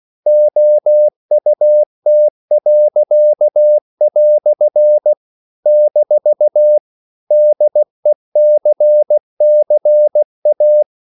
Morse code, heard as a keyed tone, about 16 wpm, 600 Hz. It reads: OUT." - DECCA
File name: CW / Practice / EnglishQuotations0416